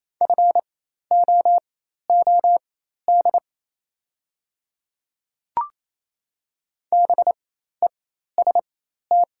Morse code, heard as a keyed tone, about 27 wpm, spaced as for 13 wpm, 690 Hz. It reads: FOOD BEST
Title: 4x Character Spacing - Top 500 Words - 28wpm